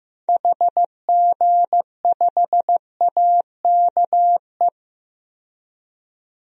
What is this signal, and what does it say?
Morse code, heard as a keyed tone, about 15 wpm, 700 Hz. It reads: HG5AKE
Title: DXCC-100 Call Signs - 6 Characters - 15wpm